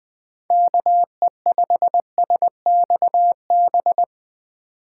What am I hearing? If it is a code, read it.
Morse code, heard as a keyed tone, about 21 wpm, 700 Hz. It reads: KE5SXB
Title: USA Call Signs - 6 Characters - 4x Character Spacing - 20wpm